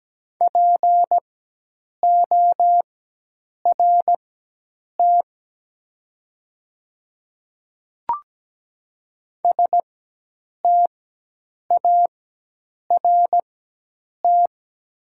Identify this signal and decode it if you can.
Morse code, heard as a keyed tone, about 17 wpm, spaced as for 8 wpm, 700 Hz. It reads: PORT START
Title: 4x Character Spacing - Top 300 Words - 17wpm